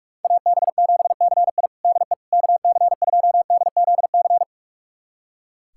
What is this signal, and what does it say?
Morse code, heard as a keyed tone, about 46 wpm, 700 Hz. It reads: AB7CI DE KC2DZC
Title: USA Callsign DE Callsign 45wpm